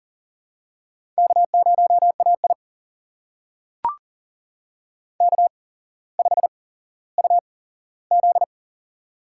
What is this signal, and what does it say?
Morse code, heard as a keyed tone, about 40 wpm, spaced as for 17 wpm, 700 Hz. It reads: K0AI K5UZ
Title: USA Call Signs - 4 Characters - 8x Character Spacing - 40wpm